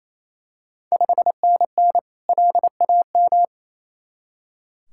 Morse code, heard as a keyed tone, about 27 wpm, 690 Hz. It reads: 5NN LAM
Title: Ontario QSO Party Exchange 28wpm